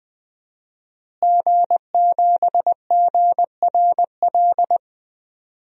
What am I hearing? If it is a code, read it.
Morse code, heard as a keyed tone, about 20 wpm, 700 Hz. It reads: G7GRL
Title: DXCC-100 Call Signs - 5 Characters - Repeated 2x - 20wpm